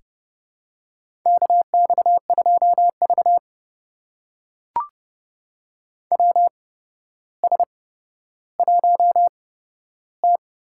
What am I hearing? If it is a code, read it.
Morse code, heard as a keyed tone, about 30 wpm, spaced as for 13 wpm, 700 Hz. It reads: KX2V WS1T
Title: USA Call Signs - 4 Characters - 8x Character Spacing - 30wpm